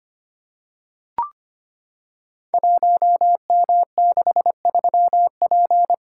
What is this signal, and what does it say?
Morse code, heard as a keyed tone, about 25 wpm, 700 Hz. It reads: 1M63P
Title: Code Groups - 5 Letter:Numbers - 25wpm